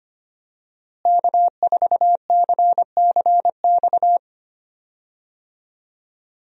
Morse code, heard as a keyed tone, about 25 wpm, 700 Hz. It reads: K4CCX
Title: USA Call Signs - 5 Characters - Repeated 3x - 25wpm